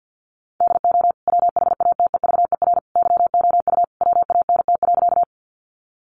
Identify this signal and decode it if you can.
Morse code, heard as a keyed tone, about 50 wpm, 700 Hz. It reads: DO WHATEVER YOU WANT.
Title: Sentences from Top 1100 Words 50wpm